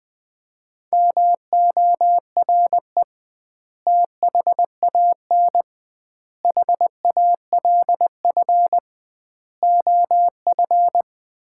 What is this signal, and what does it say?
Morse code, heard as a keyed tone, about 21 wpm, 700 Hz. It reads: MORE THAN HALF OF
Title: 2x Word-Spacing - Sets of 5 Words - 20wpm